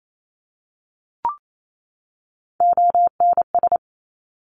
Morse code, heard as a keyed tone, about 28 wpm, 700 Hz. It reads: ONS
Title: Top Three Letter Combinations in Words 28wpm